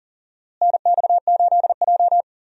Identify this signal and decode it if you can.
Morse code, heard as a keyed tone, about 40 wpm, 700 Hz. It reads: NX8J
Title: USA Call Signs - 4 Characters - Repeated 3x - 40wpm